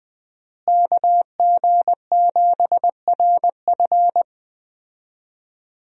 Morse code, heard as a keyed tone, about 20 wpm, 700 Hz. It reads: KG7RF
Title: USA Call Signs - 5 Characters - Repeated 3x - 20wpm